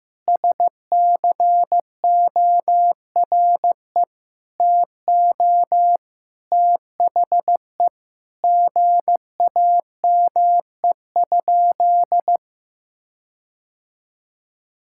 Morse code, heard as a keyed tone, about 15 wpm, 700 Hz. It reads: SCORE TO THE GAME?
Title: Sentences from Top 1600 Words 15wpm